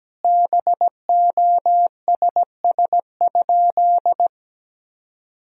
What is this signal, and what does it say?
Morse code, heard as a keyed tone, about 17 wpm, 700 Hz. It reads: BOSS?
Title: Sentences from Top 1000 Words - Review Part 4 - 17wpm